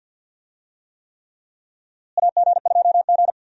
Morse code, heard as a keyed tone, about 52 wpm, 700 Hz. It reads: AG2G
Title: USA Call Signs - 4 Characters - Repeated 3x - 50wpm